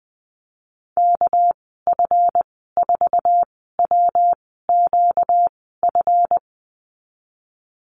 Morse code, heard as a keyed tone, about 20 wpm, spaced as for 15 wpm, 700 Hz. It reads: KF4WQF
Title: 245 Lesson 39g - Letter z - Call Signs 20wpm